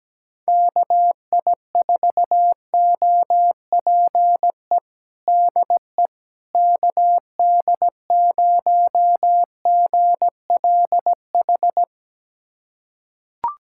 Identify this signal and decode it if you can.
Morse code, heard as a keyed tone, about 17 wpm, 700 Hz. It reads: KI4OPE DE KD0GLH
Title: USA Callsign DE Callsign 17wpm